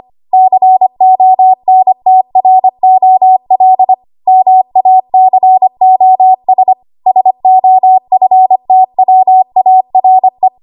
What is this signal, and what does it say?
Morse code, heard as a keyed tone, about 25 wpm, 750 Hz. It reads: CONTROL MACOS SOFTWARE
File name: Archive / 25 WPM / 25WPM